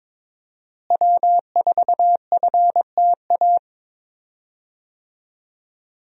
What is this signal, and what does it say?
Morse code, heard as a keyed tone, about 22 wpm, 700 Hz. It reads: W4FTA
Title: USA Call Signs - 5 Characters - Repeated 2x - 22wpm